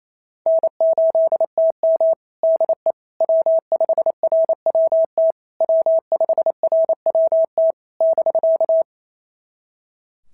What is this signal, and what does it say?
Morse code, heard as a keyed tone, about 27 wpm, 650 Hz. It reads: N8TM DE W5RWT W5RWT <BK>
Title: SOTA QSO Practice 28wpm